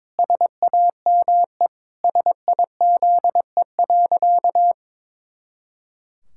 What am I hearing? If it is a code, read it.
Morse code, heard as a keyed tone, about 22 wpm, 700 Hz. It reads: SAME SIZE.
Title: Pangram Challenge v2 - 22wpm